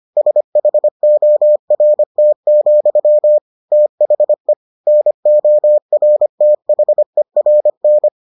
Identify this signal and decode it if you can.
Morse code, heard as a keyed tone, about 25 wpm, 600 Hz. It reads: SHORT, THE NORTHERN